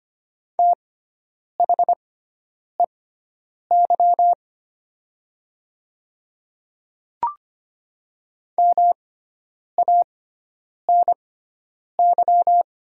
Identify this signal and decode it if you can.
Morse code, heard as a keyed tone, about 25 wpm, spaced as for 9 wpm, 690 Hz. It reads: THEY MANY
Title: 6x Character Spacing - Top 100 Words - 25wpm